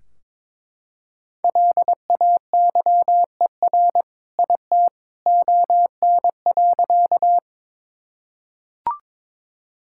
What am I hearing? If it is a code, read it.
Morse code, heard as a keyed tone, about 22 wpm, 700 Hz. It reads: LAYER IT ON.
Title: Sentences from Top 1600 Words 22wpm